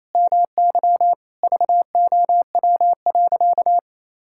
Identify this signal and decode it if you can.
Morse code, heard as a keyed tone, about 28 wpm, 700 Hz. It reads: MY VOW.
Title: Pangram Challenge v2 - 28wpm